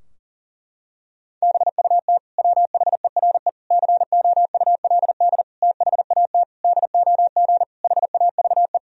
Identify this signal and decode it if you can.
Morse code, heard as a keyed tone, about 40 wpm, 700 Hz. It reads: BUT WHERE COULD THAT DOG HAVE
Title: Sentences from Top 1000 Words - Review Part 2 - 40wpm